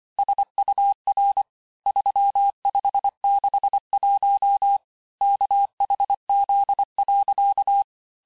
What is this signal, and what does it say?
Morse code, heard as a keyed tone, about 25 wpm, 800 Hz. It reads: SUR 3561 KHZ.
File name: texte-chiffre-vitesse-23mpm